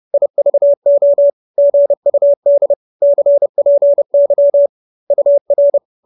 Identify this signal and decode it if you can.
Morse code, heard as a keyed tone, about 30 wpm, 570 Hz. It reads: IVO GUD CPY UR
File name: QSO319 30